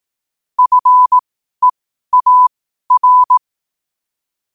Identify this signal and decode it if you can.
Morse code, heard as a keyed tone, about 18 wpm, spaced as for 12 wpm, 1000 Hz. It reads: FEAR
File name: qotdfiles / 12wpm